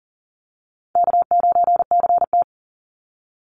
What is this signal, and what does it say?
Morse code, heard as a keyed tone, about 40 wpm, 700 Hz. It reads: K9CT
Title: Easy Call Signs Repeated 3x - 40wpm